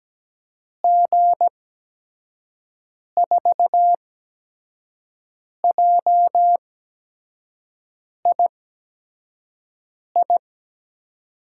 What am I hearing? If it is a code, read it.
Morse code, heard as a keyed tone, about 17 wpm, spaced as for 5 wpm, 700 Hz. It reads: G4JII
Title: USA Call Signs - 6 Characters - 8x Character Spacing - 17wpm